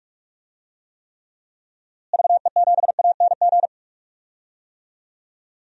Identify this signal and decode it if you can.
Morse code, heard as a keyed tone, about 41 wpm, 700 Hz. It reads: VE7ANG
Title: DXCC-100 Call Signs - 6 Characters - Repeated 3x - 45wpm